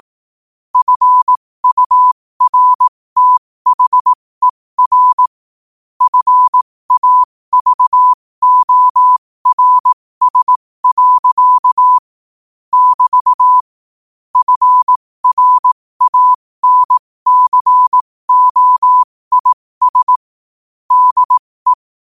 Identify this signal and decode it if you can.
Morse code, heard as a keyed tone, about 18 wpm, spaced as for 15 wpm, 1000 Hz. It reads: FURTHER FAVORS. <BT> FRANCOIS DE
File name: qotdfiles / 15wpm